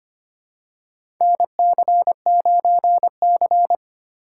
Morse code, heard as a keyed tone, about 25 wpm, 700 Hz. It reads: NC9C
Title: USA Call Signs - 4 Characters - 4x Character Spacing - 25wpm